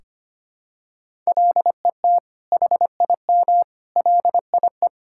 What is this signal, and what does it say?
Morse code, heard as a keyed tone, about 25 wpm, 700 Hz. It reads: LET HIM LIE
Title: Sentences from Top 1600 Words 25wpm